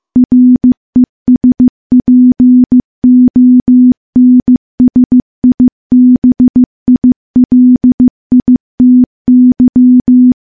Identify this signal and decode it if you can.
Morse code, heard as a keyed tone, about 15 wpm, 250 Hz. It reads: RESPONSIBILITY
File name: morse-training-15WPM